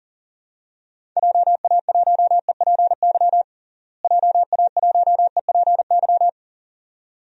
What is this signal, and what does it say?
Morse code, heard as a keyed tone, about 40 wpm, 700 Hz. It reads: JA1EPY JA1EPY
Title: DXCC-100 Call Signs - 6 Characters - Repeated 2x - 40wpm